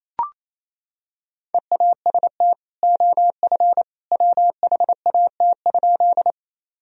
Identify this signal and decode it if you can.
Morse code, heard as a keyed tone, about 27 wpm, 700 Hz. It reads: EAST OF WHAT?
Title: Sentences from Top 1000 Words - Review Part 3 - 28wpm